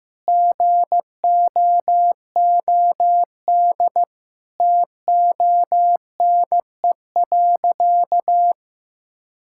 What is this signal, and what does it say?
Morse code, heard as a keyed tone, about 15 wpm, 700 Hz. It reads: GOOD TONE.
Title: Sentences from Top 1600 Words 15wpm